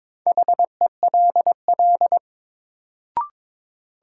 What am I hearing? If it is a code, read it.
Morse code, heard as a keyed tone, about 22 wpm, 700 Hz. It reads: HELL
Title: Sets of 5 Words Encore - New Word Drill - 22wpm